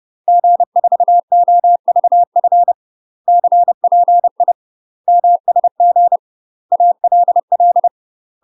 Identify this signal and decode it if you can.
Morse code, heard as a keyed tone, about 30 wpm, 700 Hz. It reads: G4OVF CPI MSG ALL
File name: UK MASTER QSOs TRAINING / 30WPM / UK MASTER QSOs 0038